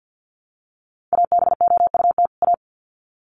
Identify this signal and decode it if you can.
Morse code, heard as a keyed tone, about 50 wpm, 700 Hz. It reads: ABOUT A
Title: Sets of 2 Words 50wpm